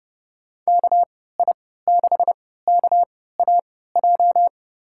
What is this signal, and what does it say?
Morse code, heard as a keyed tone, about 30 wpm, spaced as for 17 wpm, 700 Hz. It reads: KI6KAJ
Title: USA Call Signs - 6 Characters - 3x Character Spacing - 30wpm